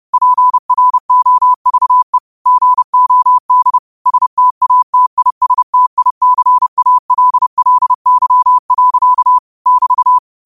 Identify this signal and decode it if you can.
Morse code, heard as a keyed tone, about 30 wpm, 1000 Hz. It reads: PROVE GOD STATISTICALLY. <BT>
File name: qotdfiles / 30wpm